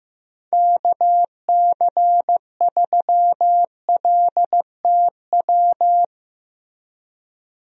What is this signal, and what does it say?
Morse code, heard as a keyed tone, about 15 wpm, 700 Hz. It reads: KC3LTW
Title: USA Call Signs - 6 Characters - Repeated 3x - 15wpm